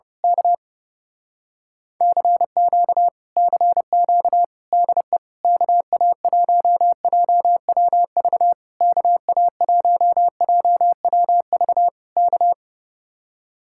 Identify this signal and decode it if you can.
Morse code, heard as a keyed tone, about 28 wpm, 700 Hz. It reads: K CQ CQ DE KA1JWV KA1JWV K
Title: Speed-Racing Common QSO Phrases Encore - 30wpm